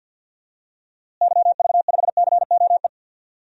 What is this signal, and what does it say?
Morse code, heard as a keyed tone, about 52 wpm, 700 Hz. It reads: YV5COE
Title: DXCC-100 Call Signs - 6 Characters - Repeated 3x - 50wpm